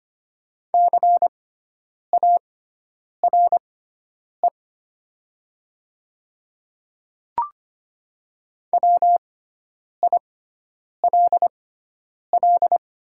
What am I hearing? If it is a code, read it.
Morse code, heard as a keyed tone, about 25 wpm, spaced as for 9 wpm, 700 Hz. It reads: CARE WILL
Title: 6x Character Spacing - Top 500 Words - 25wpm